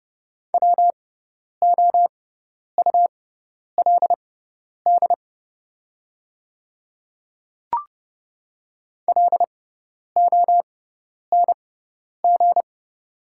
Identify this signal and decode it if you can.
Morse code, heard as a keyed tone, about 30 wpm, spaced as for 10 wpm, 700 Hz. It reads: WOULD LONG
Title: 6x Character Spacing - Top 100 Words - 30wpm